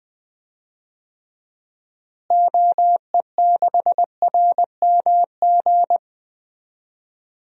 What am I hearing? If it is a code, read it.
Morse code, heard as a keyed tone, about 20 wpm, 700 Hz. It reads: OE6RMG
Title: DXCC-100 Call Signs - 6 Characters - Repeated 2x - 20wpm